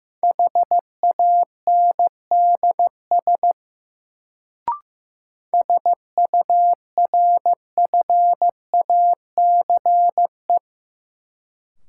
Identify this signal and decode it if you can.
Morse code, heard as a keyed tone, about 15 wpm, 700 Hz. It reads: HANDS SURFACE
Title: Sets of 4 Words - New Word Drill - 15wpm